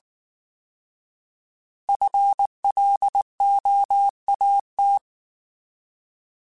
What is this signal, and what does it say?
Morse code, heard as a keyed tone, about 19 wpm, 800 Hz. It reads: FLOAT
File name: Alphabetical / Alphabetical mp3s / Fs @18